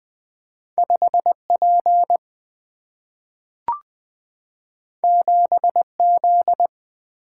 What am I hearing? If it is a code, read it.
Morse code, heard as a keyed tone, about 20 wpm, 700 Hz. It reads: 5P 7Z
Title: Single-Letter Single-Number 20wpm